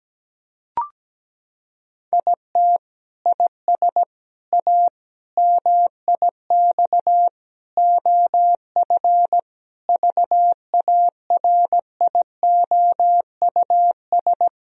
Morse code, heard as a keyed tone, about 17 wpm, 700 Hz. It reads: IT IS A MIX OF VARIOUS
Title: Sentences from Top 1000 Words - Review Part 4 - 17wpm